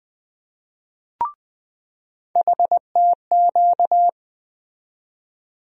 Morse code, heard as a keyed tone, about 20 wpm, 700 Hz. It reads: HTQ
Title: USA Call Sign Postfixes - 3 Letters 20wpm